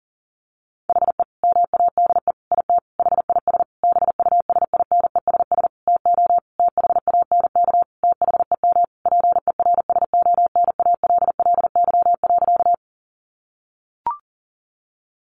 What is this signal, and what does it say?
Morse code, heard as a keyed tone, about 37 wpm, 700 Hz. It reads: HE MADE IT HIS BUSINESS TO THANK THEM PERSONALLY.
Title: Sentences from Top 1400 Words 40wpm